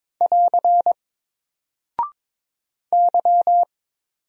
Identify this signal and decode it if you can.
Morse code, heard as a keyed tone, about 23 wpm, 700 Hz. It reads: <AR> Y
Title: Single Character (Letter:Number:Punctuation:Prosign) Alt 22wpm